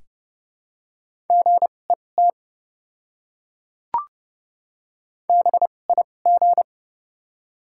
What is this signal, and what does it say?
Morse code, heard as a keyed tone, about 30 wpm, spaced as for 22 wpm, 700 Hz. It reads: GET BIG
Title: 2x Character Spacing - Top 200 Words - 30wpm